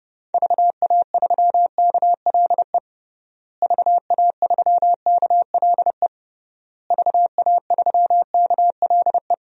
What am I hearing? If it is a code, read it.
Morse code, heard as a keyed tone, about 30 wpm, 700 Hz. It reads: VA3KLE VA3KLE VA3KLE
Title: DXCC-100 Call Signs - 6 Characters - Repeated 3x - 30wpm